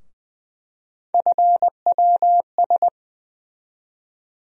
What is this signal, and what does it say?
Morse code, heard as a keyed tone, about 20 wpm, 700 Hz. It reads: FWS
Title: USA Call Sign Postfixes - 3 Letters 20wpm